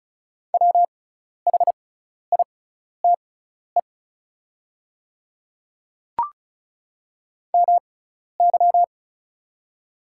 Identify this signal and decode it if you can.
Morse code, heard as a keyed tone, about 34 wpm, spaced as for 12 wpm, 700 Hz. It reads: WHITE MY
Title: 6x Character Spacing - Top 500 Words - 35wpm